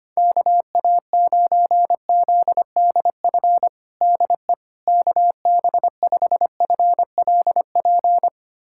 Morse code, heard as a keyed tone, about 25 wpm, 700 Hz. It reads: KA9ZDF DE KB5FLP
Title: USA Callsign DE Callsign - 6 Characters - 25wpm